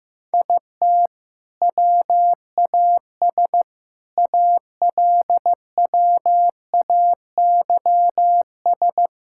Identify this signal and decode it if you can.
Morse code, heard as a keyed tone, about 15 wpm, 700 Hz. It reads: IT WAS ALWAYS
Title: Sentences from Top 1000 Words - Review Part 5 - 15wpm